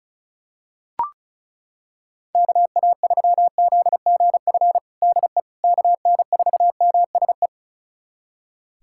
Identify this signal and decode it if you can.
Morse code, heard as a keyed tone, about 36 wpm, 700 Hz. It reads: KA3ZGF DE KN4MSE
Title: USA Callsign DE Callsign 35wpm